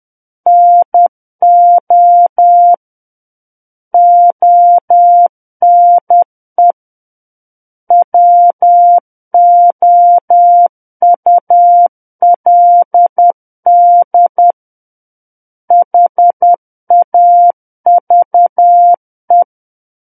Morse code, heard as a keyed tone, about 10 wpm, 700 Hz. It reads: NO ONE WOULD HAVE
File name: Book One / 01-Chapter 1 - 10 WPM